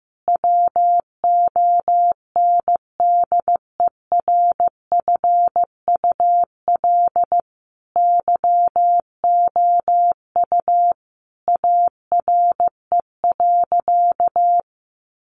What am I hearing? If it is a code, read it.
Morse code, heard as a keyed tone, about 15 wpm, 700 Hz. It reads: WONDERFUL YOU ARE.
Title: Sentences from Top 1100 Words 15wpm